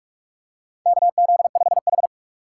Morse code, heard as a keyed tone, about 47 wpm, 700 Hz. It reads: KZ5H
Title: USA Call Signs - 6x Character Spacing - 45wpm